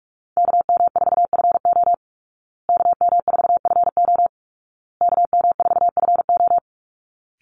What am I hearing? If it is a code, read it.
Morse code, heard as a keyed tone, about 44 wpm, 700 Hz. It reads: KM4FO KM4FO KM4FO
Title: Call Signs Repeated 2x - 45wpm